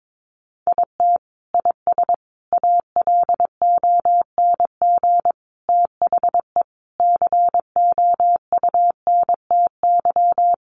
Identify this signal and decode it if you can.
Morse code, heard as a keyed tone, about 22 wpm, 700 Hz. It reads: IT IS ALONG THE COUNTY
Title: Sentences from Top 1200 Words 22wpm